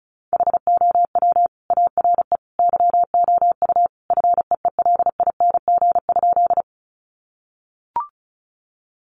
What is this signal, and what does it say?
Morse code, heard as a keyed tone, about 34 wpm, 700 Hz. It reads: HOW ARE YOU FEELING?
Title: Sentences from Top 1200 Words 35wpm